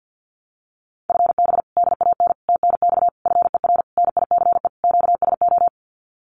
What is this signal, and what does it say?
Morse code, heard as a keyed tone, about 50 wpm, 700 Hz. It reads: FB DAN TNX FER NICE QSO
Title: Common QSO Phrases Encore 50wpm